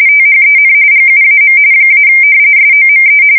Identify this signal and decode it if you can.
RTTY, about 45.45 baud. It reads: THEN GUAM - SOME
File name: fsk170 / battleshipnj